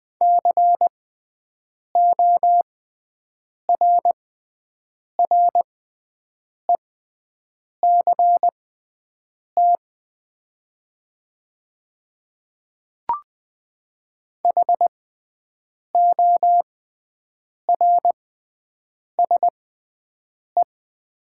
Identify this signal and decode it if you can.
Morse code, heard as a keyed tone, about 20 wpm, spaced as for 7 wpm, 700 Hz. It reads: CORRECT HORSE
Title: 6x Character Spacing - Top 500 Words - 20wpm